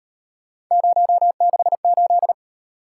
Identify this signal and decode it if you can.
Morse code, heard as a keyed tone, about 38 wpm, 700 Hz. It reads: 068